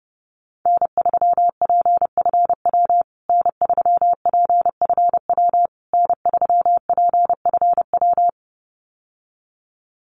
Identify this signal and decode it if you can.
Morse code, heard as a keyed tone, about 30 wpm, 700 Hz. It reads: N3PFW N3PFW N3PFW
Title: USA Call Signs Repeated 3x 30wpm